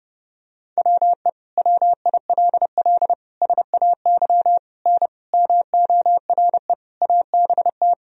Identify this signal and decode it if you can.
Morse code, heard as a keyed tone, about 30 wpm, 700 Hz. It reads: WE WILL SAY N MORE ABT
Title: Sentences from Top 200 Words and Abbreviations 30wpm